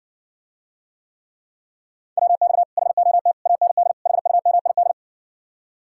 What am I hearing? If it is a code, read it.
Morse code, heard as a keyed tone, about 60 wpm, 700 Hz. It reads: WX HOT AND HUMID